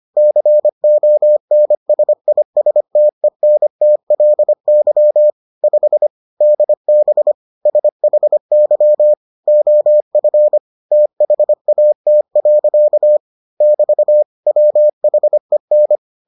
Morse code, heard as a keyed tone, about 25 wpm, 600 Hz. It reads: CONSISTENTLY 5 DB SHY OF THAT. <BT> WHEN